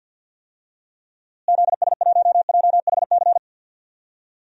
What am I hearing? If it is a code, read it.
Morse code, heard as a keyed tone, about 52 wpm, 700 Hz. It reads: ZS1JHC
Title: DXCC-100 Call Signs - 6 Characters - Repeated 3x - 50wpm